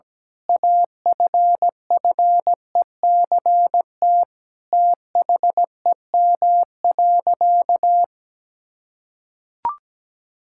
Morse code, heard as a keyed tone, about 17 wpm, 700 Hz. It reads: AFFECT THEM.
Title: Sentences from Top 1000 Words - Review Part 6 - 17wpm